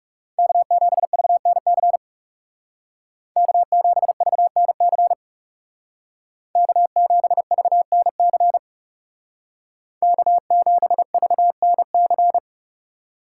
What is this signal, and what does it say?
Morse code, heard as a keyed tone, about 37 wpm, 700 Hz. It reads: K7VNC K7VNC K7VNC K7VNC
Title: Speed Racing - USA 1x3 Call Signs 30wpm